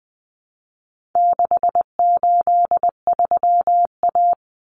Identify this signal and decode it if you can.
Morse code, heard as a keyed tone, about 20 wpm, 700 Hz. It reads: 683A
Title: Common Cut Numbers - 4-Digit 20wpm